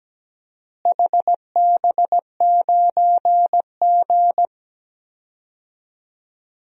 Morse code, heard as a keyed tone, about 17 wpm, 700 Hz. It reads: HB9G
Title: DXCC-100 Call Signs - 4 Characters - Repeated 2x - 17wpm